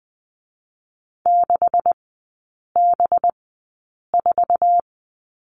Morse code, heard as a keyed tone, about 20 wpm, 700 Hz. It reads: 6 B 4
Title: 184 Lesson 30e - Letter b - Three Characters 20wpm